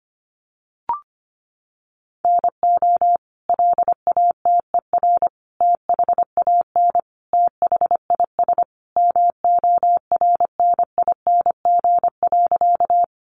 Morse code, heard as a keyed tone, about 25 wpm, 700 Hz. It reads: NO LATER THAN THIS MORNING.
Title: Sentences from Top 700 Words 25wpm